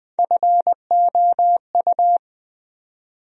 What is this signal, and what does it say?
Morse code, heard as a keyed tone, about 21 wpm, 700 Hz. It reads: FOU
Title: USA Call Sign Postfixes - 3 Letters 20wpm